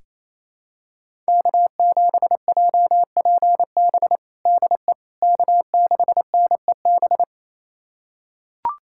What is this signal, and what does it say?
Morse code, heard as a keyed tone, about 28 wpm, 700 Hz. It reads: K7JPB DE K6NEB
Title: USA Callsign DE Callsign 28wpm